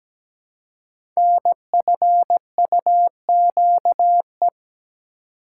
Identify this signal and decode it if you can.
Morse code, heard as a keyed tone, about 17 wpm, 700 Hz. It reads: NFUQE
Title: Code Groups - 5 Letter:Numbers - 17wpm